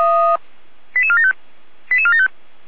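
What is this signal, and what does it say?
Five-tone selective call (ZVEI), heard as quick runs of five tones, tones of about 70 ms. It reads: 88475 88475
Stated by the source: Five-tone ZVEI selective call received by SDR, from server of Feuerwehr Orlamünde (fire brigade).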